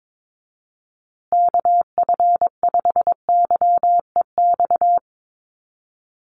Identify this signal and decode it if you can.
Morse code, heard as a keyed tone, about 22 wpm, 700 Hz. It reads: KF5YEX
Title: USA Call Signs Repeated 3x 22wpm